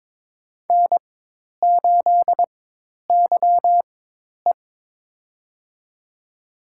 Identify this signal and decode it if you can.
Morse code, heard as a keyed tone, about 22 wpm, spaced as for 10 wpm, 700 Hz. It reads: N8YE
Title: USA Call Signs - 4 Characters - 4x Character Spacing - 22wpm